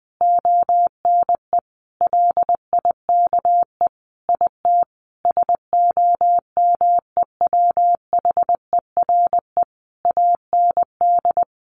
Morse code, heard as a keyed tone, about 20 wpm, 700 Hz. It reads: ONE LIKE IT SOMEWHERE AND
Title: Sentences from Top 1400 Words 20wpm